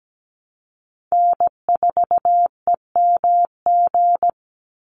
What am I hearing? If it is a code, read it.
Morse code, heard as a keyed tone, about 17 wpm, 700 Hz. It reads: N4EMG
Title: NA SOTA Chaser Callsigns 17wpm